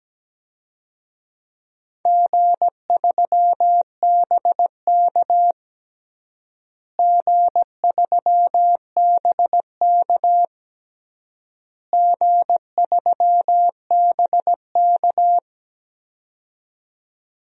Morse code, heard as a keyed tone, about 17 wpm, 700 Hz. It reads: G3BK G3BK G3BK
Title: DXCC-100 Call Signs - 4 Characters - Repeated 3x - 17wpm